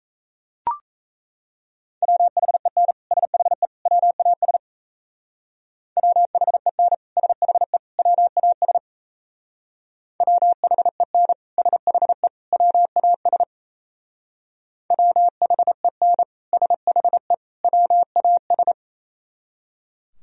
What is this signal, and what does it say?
Morse code, heard as a keyed tone, about 34 wpm, 700 Hz. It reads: WHEN SHE WAS WHEN SHE WAS WHEN SHE WAS WHEN SHE WAS
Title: Speed-Racing - Sets of 3 Words 28wpm